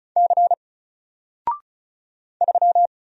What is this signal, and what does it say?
Morse code, heard as a keyed tone, about 35 wpm, 700 Hz. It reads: C 3
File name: Single-Letter-Number-35wpm